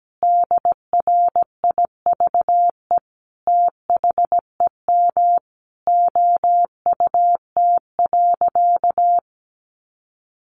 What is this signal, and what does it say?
Morse code, heard as a keyed tone, about 17 wpm, 700 Hz. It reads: DRIVE THEM OUT.
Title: Sentences from Top 500 Words 17wpm